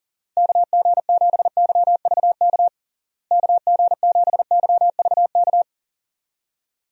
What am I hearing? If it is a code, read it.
Morse code, heard as a keyed tone, about 40 wpm, 700 Hz. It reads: KG7YVK KG7YVK
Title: USA Call Signs - 6 Characters - Repeated 2x - 40wpm